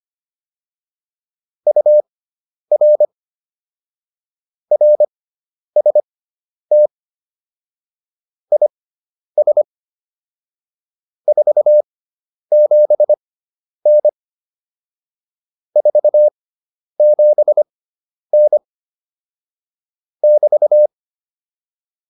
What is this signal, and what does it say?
Morse code, heard as a keyed tone, about 25 wpm, spaced as for 10 wpm, 600 Hz. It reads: UR RST IS 47N 47N <BT>
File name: QSO209 10